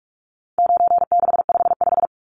Morse code, heard as a keyed tone, about 44 wpm, 700 Hz. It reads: 9655
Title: Numbers - 4-Digit 45wpm